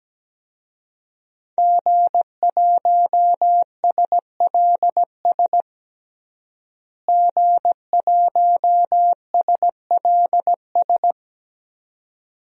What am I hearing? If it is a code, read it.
Morse code, heard as a keyed tone, about 17 wpm, 700 Hz. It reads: G1SLS G1SLS
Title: DXCC-100 Call Signs - 5 Characters - Repeated 2x - 17wpm